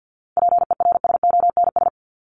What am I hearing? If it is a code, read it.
Morse code, heard as a keyed tone, about 50 wpm, 700 Hz. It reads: PERSONS
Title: ES:S Suffix - Top 100 Words with Base Words 50wpm